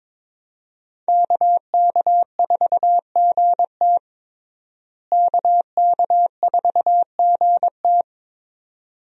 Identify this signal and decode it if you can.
Morse code, heard as a keyed tone, about 22 wpm, 700 Hz. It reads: KK4GT KK4GT
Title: USA Call Signs - 5 Characters - Repeated 2x - 22wpm